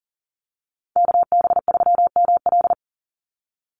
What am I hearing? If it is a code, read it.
Morse code, heard as a keyed tone, about 40 wpm, 700 Hz. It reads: KB3ML
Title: Call Signs - 40wpm